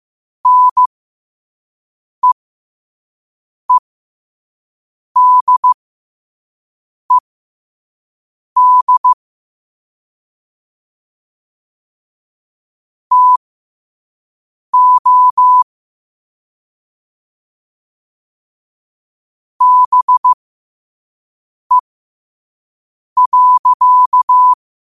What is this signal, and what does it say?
Morse code, heard as a keyed tone, about 15 wpm, spaced as for 5 wpm, 1000 Hz. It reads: NEEDED TO BE.
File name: qotdfiles / 5wpm